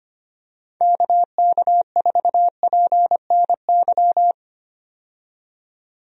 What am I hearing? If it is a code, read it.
Morse code, heard as a keyed tone, about 25 wpm, 700 Hz. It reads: KK4PNY
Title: USA Call Signs - 6 Characters - 25wpm